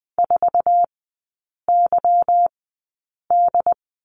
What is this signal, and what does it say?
Morse code, heard as a keyed tone, about 20 wpm, spaced as for 8 wpm, 690 Hz. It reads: 4YD
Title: 145 Lesson 24e - Letter y - Three Characters 20wpm